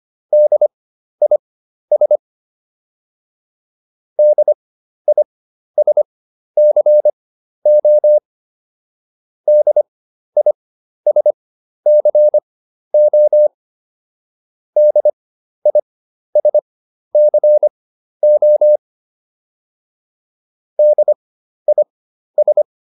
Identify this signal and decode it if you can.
Morse code, heard as a keyed tone, about 25 wpm, spaced as for 12 wpm, 600 Hz. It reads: DIS DISCO DISCO DISCO DIS